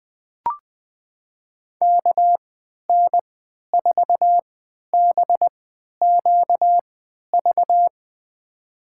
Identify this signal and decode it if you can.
Morse code, heard as a keyed tone, about 20 wpm, spaced as for 11 wpm, 700 Hz. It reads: KN4BQV
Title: USA Call Signs - 6 Characters - 3x Character Spacing - 20wpm